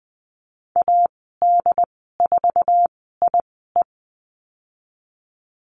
Morse code, heard as a keyed tone, about 20 wpm, spaced as for 14 wpm, 700 Hz. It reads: AD4IE
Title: 069 Lesson 12g - Letter d - Call Signs 20wpm